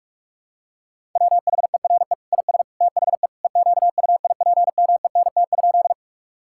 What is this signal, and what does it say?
Morse code, heard as a keyed tone, about 46 wpm, 690 Hz. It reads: WHERE IS THE EQUIPMENT?
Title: Sentences from Top 1000 Words - Review Part 5 - 45wpm